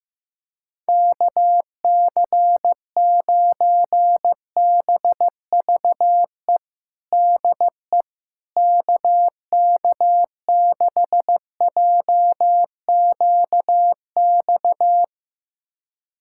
Morse code, heard as a keyed tone, about 15 wpm, 700 Hz. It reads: KC9BVE DE KK6JQX
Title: USA Callsign DE Callsign - 6 Characters - 15wpm